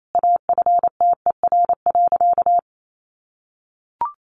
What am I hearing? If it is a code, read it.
Morse code, heard as a keyed tone, about 28 wpm, 700 Hz. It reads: AFTER.
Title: Sentences from Top 1500 Words 28wpm